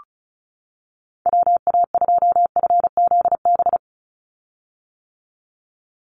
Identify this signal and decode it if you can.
Morse code, heard as a keyed tone, about 34 wpm, 700 Hz. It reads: WA2FZB
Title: Call Signs - 35wpm